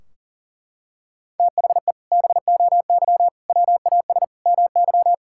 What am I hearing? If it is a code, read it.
Morse code, heard as a keyed tone, about 37 wpm, 700 Hz. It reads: THE BOY WAS MY
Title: Sentences from Top 1600 Words 40wpm